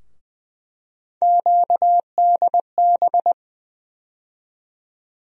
Morse code, heard as a keyed tone, about 20 wpm, 700 Hz. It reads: QDB
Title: USA Call Sign Postfixes - 3 Letters 20wpm